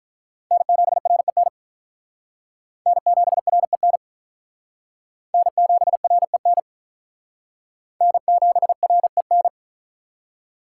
Morse code, heard as a keyed tone, about 42 wpm, 700 Hz. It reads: N7REN N7REN N7REN N7REN
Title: Speed Racing - USA 1x3 Call Signs 35wpm